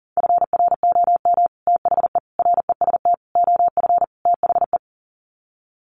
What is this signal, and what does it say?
Morse code, heard as a keyed tone, about 40 wpm, 700 Hz. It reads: FROM THE REST OF THE
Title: Sets of 5 Words 40wpm